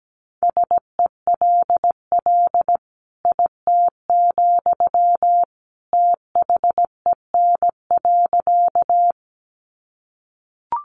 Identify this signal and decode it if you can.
Morse code, heard as a keyed tone, about 17 wpm, 700 Hz. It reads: SELL IT, THEN.
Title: Sentences from Top 1100 Words 17wpm